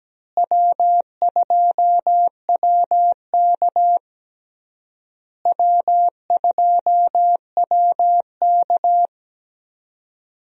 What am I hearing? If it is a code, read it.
Morse code, heard as a keyed tone, about 17 wpm, 700 Hz. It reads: W2WK W2WK
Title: USA Call Signs - 4 Characters - Repeated 3x - 17wpm